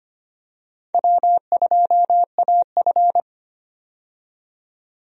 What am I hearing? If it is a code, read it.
Morse code, heard as a keyed tone, about 25 wpm, 700 Hz. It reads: W2AF
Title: USA Call Signs - 4 Characters - 25wpm